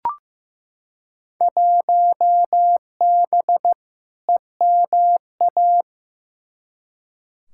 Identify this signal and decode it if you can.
Morse code, heard as a keyed tone, about 15 wpm, 700 Hz. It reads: 1B EMA
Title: Field Day Exchanges - 15wpm